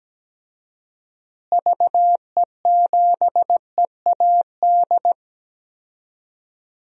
Morse code, heard as a keyed tone, about 17 wpm, 700 Hz. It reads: VE7EAD
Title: DXCC-100 Call Signs - 6 Characters - Repeated 3x - 17wpm